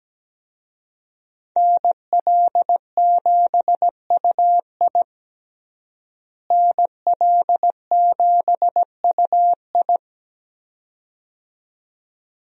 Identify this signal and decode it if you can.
Morse code, heard as a keyed tone, about 17 wpm, 700 Hz. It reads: NL7UI NL7UI
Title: USA Call Signs - 5 Characters - Repeated 2x - 17wpm